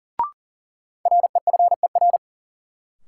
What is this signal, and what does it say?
Morse code, heard as a keyed tone, about 39 wpm, 700 Hz. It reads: REFER
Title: Sentences from Top 1000 Words - New Word Drill - 40wpm